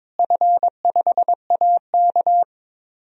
Morse code, heard as a keyed tone, about 22 wpm, 700 Hz. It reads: F5AK
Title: DXCC-100 Call Signs - 4 Characters - Repeated 2x - 22wpm